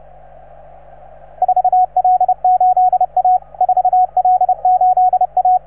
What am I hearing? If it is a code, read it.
Morse code, heard as a keyed tone, about 30 wpm, 750 Hz, with radio noise. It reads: 4L8A 4L8A